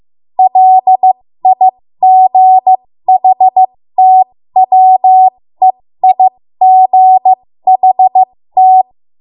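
Morse code, heard as a keyed tone, about 15 wpm, spaced as for 13 wpm, 750 Hz. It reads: LIGHTWEIGHT
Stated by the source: 13 wpm Morse code practice archive transmission.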